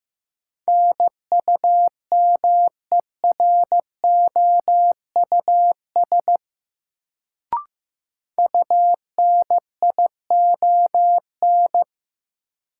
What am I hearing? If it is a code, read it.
Morse code, heard as a keyed tone, about 15 wpm, 700 Hz. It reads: NUMEROUS UNION
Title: Sentences from Top 1400 Words - New Word Drill Alt - 15wpm